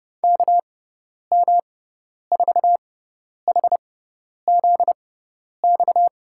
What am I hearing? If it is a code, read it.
Morse code, heard as a keyed tone, about 31 wpm, spaced as for 10 wpm, 700 Hz. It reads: KM4HZX
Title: USA Call Signs - 6 Characters - 6x Character Spacing - 30wpm